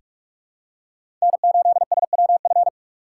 Morse code, heard as a keyed tone, about 46 wpm, 700 Hz. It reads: N8SWF
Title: USA Call Signs - 5 Characters - 4x Character Spacing - 45wpm